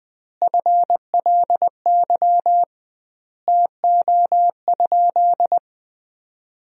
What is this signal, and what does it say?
Morse code, heard as a keyed tone, about 21 wpm, 700 Hz. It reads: FLY TO?
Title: 2x Word-Spacing - Sentences from Top 500 Words - 20wpm